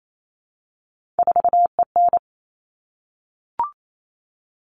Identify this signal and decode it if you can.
Morse code, heard as a keyed tone, about 28 wpm, 700 Hz. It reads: 4EN
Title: All Cut Numbers - 3-Digit 28wpm